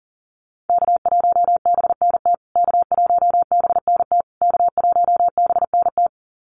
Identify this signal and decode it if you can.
Morse code, heard as a keyed tone, about 40 wpm, 700 Hz. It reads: K1BNT K1BNT K1BNT
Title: USA Call Signs Repeated 3x 40wpm